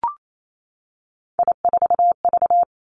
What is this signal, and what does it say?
Morse code, heard as a keyed tone, about 28 wpm, 700 Hz. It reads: I4V
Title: Three Challenging Letter-Number Combinations 28wpm